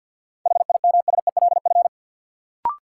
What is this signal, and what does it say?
Morse code, heard as a keyed tone, about 53 wpm, 700 Hz. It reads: HIMSELF
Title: Sentences from Top 1500 Words - New Word Drill - 50wpm